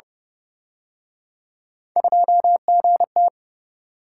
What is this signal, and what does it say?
Morse code, heard as a keyed tone, about 31 wpm, 700 Hz. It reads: 2GT